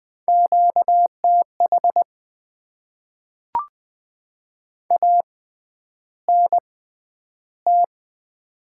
Morse code, heard as a keyed tone, about 20 wpm, 700 Hz. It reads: Q T H A N T
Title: Top 20 QSO Elements - 6x Character Spacing - 20wpm